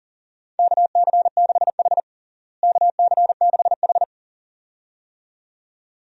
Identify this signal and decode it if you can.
Morse code, heard as a keyed tone, about 40 wpm, 700 Hz. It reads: KC6H KC6H
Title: USA Call Signs - 4 Characters - Repeated 2x - 40wpm